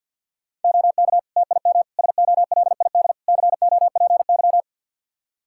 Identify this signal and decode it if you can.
Morse code, heard as a keyed tone, about 47 wpm, 700 Hz. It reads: OK TIM SOLID COPY
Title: Practice QSO - Exchange 3 - More Info - 50wpm